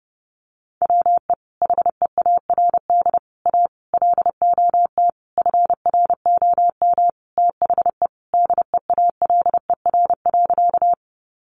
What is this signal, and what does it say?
Morse code, heard as a keyed tone, about 30 wpm, 700 Hz. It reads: WE HEARD A LOT FROM THE DEALER.
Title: Sentences from Top 1100 Words 30wpm